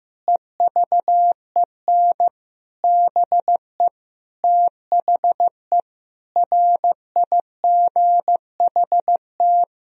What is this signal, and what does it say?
Morse code, heard as a keyed tone, about 16 wpm, 700 Hz. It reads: EVEN BE THE RIGHT
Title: Sentences from Top 1000 Words - Review Part 3 - 15wpm